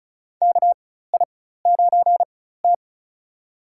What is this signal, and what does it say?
Morse code, heard as a keyed tone, about 36 wpm, spaced as for 16 wpm, 700 Hz. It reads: KI9T
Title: USA Call Signs - 4 Characters - 4x Character Spacing - 35wpm